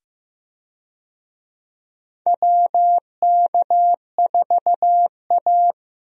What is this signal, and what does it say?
Morse code, heard as a keyed tone, about 15 wpm, 700 Hz. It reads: WK4A
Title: USA Call Signs - 4 Characters - 15wpm